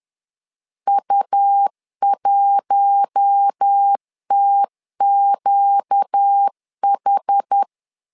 Morse code, heard as a keyed tone, about 10 wpm, 800 Hz. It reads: U1TQH
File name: LEZIONI / 5 LBVC / 5 LBVC 10wpm 2349